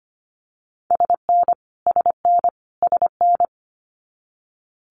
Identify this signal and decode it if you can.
Morse code, heard as a keyed tone, about 25 wpm, 700 Hz. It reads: SN SN SN
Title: Multi-letter Phonemes Repeated 3x 25wpm